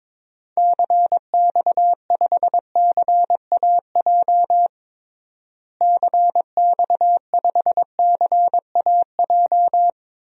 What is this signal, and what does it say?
Morse code, heard as a keyed tone, about 22 wpm, 700 Hz. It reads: CX5CAJ CX5CAJ
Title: DXCC-100 Call Signs - 6 Characters - Repeated 2x - 22wpm